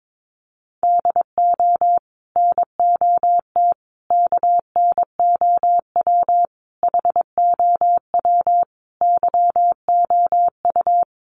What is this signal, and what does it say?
Morse code, heard as a keyed tone, about 22 wpm, 700 Hz. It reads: DO NOT KNOW HOW YOU
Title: Sets of 5 Words Encore 22wpm